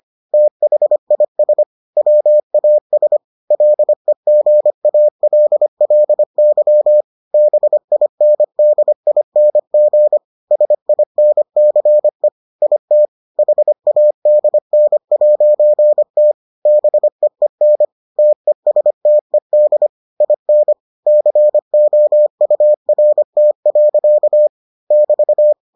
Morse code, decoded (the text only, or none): THIS WAS LEGALLY BINDING SINCE IT HADN'T BEEN TESTED IN COURT. <BT>